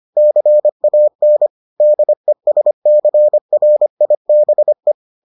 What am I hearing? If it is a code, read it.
Morse code, heard as a keyed tone, about 25 wpm, 600 Hz. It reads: CAN DESCRIBE